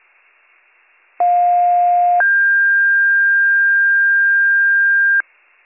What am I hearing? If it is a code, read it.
Two-tone page, heard as a one-second tone, then a three-second tone, about 710 then 1600 Hz. It is heard from 1.2 to 5.2 s.